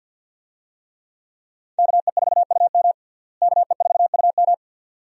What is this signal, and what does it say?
Morse code, heard as a keyed tone, about 52 wpm, 700 Hz. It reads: KE4UM KE4UM
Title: USA Call Signs - 5 Characters - Repeated 2x - 50wpm